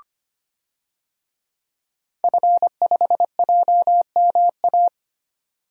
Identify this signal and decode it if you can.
Morse code, heard as a keyed tone, about 25 wpm, 700 Hz. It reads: F5JMA
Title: DXCC-100 Call Signs - 5 Characters - 25wpm